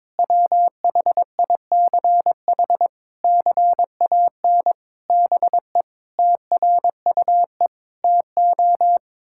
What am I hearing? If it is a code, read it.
Morse code, heard as a keyed tone, about 22 wpm, 700 Hz. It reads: WHICH CAN BE TRUE TO
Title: Sentences from Top 1600 Words 22wpm